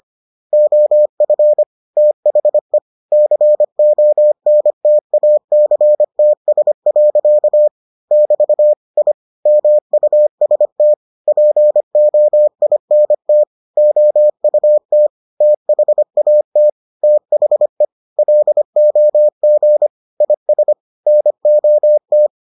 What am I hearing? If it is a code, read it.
Morse code, heard as a keyed tone, about 25 wpm, 600 Hz. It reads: OF THE CONTACTS. <BT> I MUST POINT OUT THAT THE LOG IS NOT